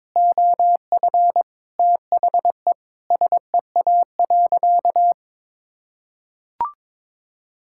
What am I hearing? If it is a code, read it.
Morse code, heard as a keyed tone, about 22 wpm, 700 Hz. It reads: OF THE SEA.
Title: Sentences from Top 1000 Words - Review Part 5 - 22wpm